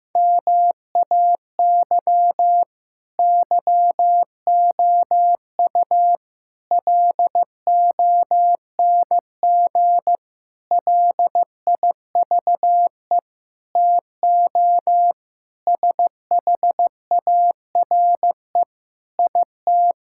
Morse code, heard as a keyed tone, about 15 wpm, 700 Hz. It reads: MAY YOU LONG LIVE TO SHARE IT
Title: Sentences from Top 1000 Words - Review Part 6 - 15wpm